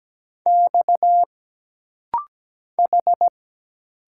Morse code, heard as a keyed tone, about 17 wpm, 700 Hz. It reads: X H
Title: Single Letters - Rapid-Fire 17wpm